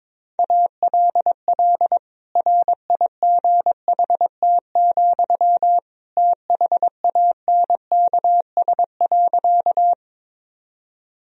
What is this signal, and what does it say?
Morse code, heard as a keyed tone, about 22 wpm, 700 Hz. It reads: ALL RIGHT, THANKS.
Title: Sentences from Top 1000 Words - Review Part 3 - 22wpm